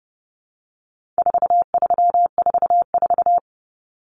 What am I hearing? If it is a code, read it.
Morse code, heard as a keyed tone, about 30 wpm, 700 Hz. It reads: 4344
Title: Common Cut Numbers - 4-Digit 30wpm